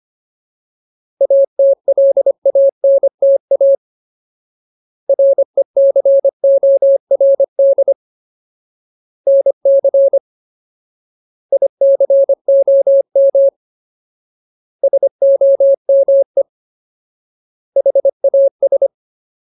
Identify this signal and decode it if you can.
Morse code, heard as a keyed tone, about 25 wpm, 550 Hz. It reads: ATLANTA RECORD NC ICOM SOME HAS
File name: Ultimate Word List / 25WPM Ultimate